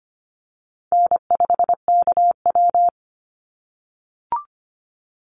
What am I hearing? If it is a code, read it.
Morse code, heard as a keyed tone, about 25 wpm, 700 Hz. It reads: N5KW
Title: Easy Call Signs - 25wpm